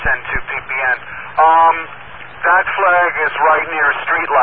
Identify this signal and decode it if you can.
Emergency Alert System (SAME) tones, heard as harsh buzzing bursts, with speech. It is heard from 0.8 to 4.2 s.